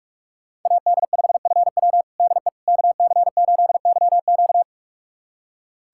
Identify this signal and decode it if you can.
Morse code, heard as a keyed tone, about 46 wpm, 700 Hz. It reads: AD5FW DE KC8YQ
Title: USA Callsign DE Callsign - 5 Characters - 45wpm